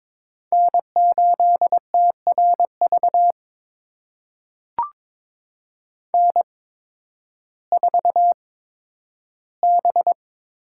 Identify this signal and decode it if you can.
Morse code, heard as a keyed tone, about 22 wpm, spaced as for 12 wpm, 700 Hz. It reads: N8TRV N4B
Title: USA Call Signs - 5 Characters - 8x Character Spacing - 22wpm